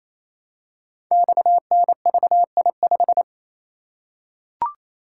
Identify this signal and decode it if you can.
Morse code, heard as a keyed tone, about 27 wpm, 700 Hz. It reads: XNVI5
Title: Code Groups - 5 Letter:Numbers - 28wpm